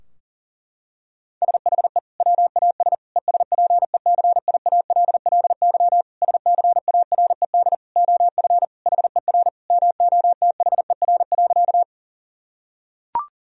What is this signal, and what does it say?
Morse code, heard as a keyed tone, about 37 wpm, 700 Hz. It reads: SHE WAS ESPECIALLY SCARED OF HER MOTHER.
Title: Sentences from Top 1000 Words - Review Part 4 - 40wpm